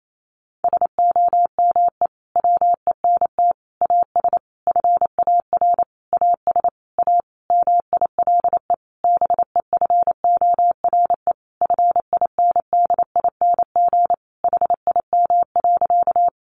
Morse code, heard as a keyed tone, about 28 wpm, 700 Hz. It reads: SOME WENT AS FAR AS A MILE BEFORE FINDING HIM.
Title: Sentences from Top 1100 Words 28wpm